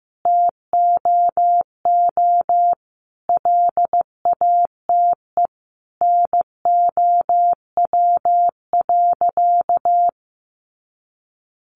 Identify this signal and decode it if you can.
Morse code, heard as a keyed tone, about 15 wpm, 700 Hz. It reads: TOO LATE NOW.
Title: Sentences from Top 300 Words 15wpm